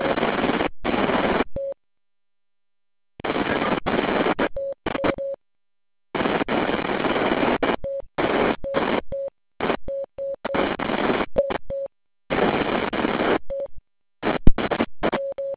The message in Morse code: none